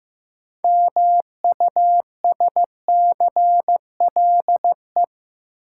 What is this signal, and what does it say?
Morse code, heard as a keyed tone, about 15 wpm, 700 Hz. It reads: MUSCLE
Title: Sentences from Top 1300 Words - New Word Drill Alt - 15wpm